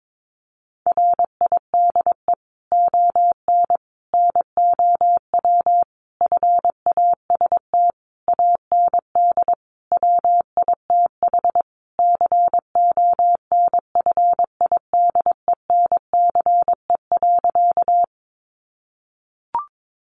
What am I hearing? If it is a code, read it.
Morse code, heard as a keyed tone, about 22 wpm, 700 Hz. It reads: RIDE ON NOW FAST AND WITH CONFIDENCE.
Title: Sentences from Top 1500 Words 22wpm